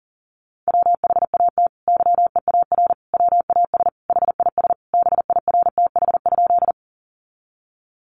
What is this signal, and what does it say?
Morse code, heard as a keyed tone, about 40 wpm, 700 Hz. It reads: WHAT YEAR WAS HIS BIRTH?
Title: Sentences from Top 800 Words 40wpm